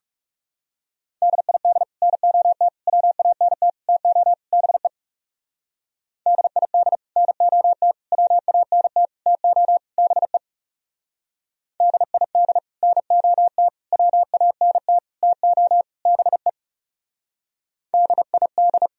Morse code, heard as a keyed tone, about 38 wpm, 700 Hz. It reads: DID NOT WANT TO BE DID NOT WANT TO BE DID NOT WANT TO BE DID